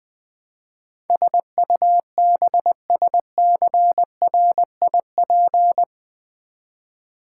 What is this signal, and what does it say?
Morse code, heard as a keyed tone, about 20 wpm, 700 Hz. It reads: SUBSCRIP
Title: Build A Word - 4 Letters At A Time - 12-Letter Words - 20wpm